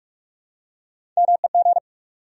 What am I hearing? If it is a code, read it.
Morse code, heard as a keyed tone, about 46 wpm, 700 Hz. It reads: MEG
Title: USA Common Names - Max 3 Letters - 45wpm